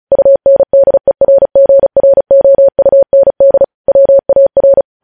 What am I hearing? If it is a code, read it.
Morse code, heard as a keyed tone, about 36 wpm, 550 Hz. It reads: UNDERGROUND WAR
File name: podcast / News headlines-1x-35x35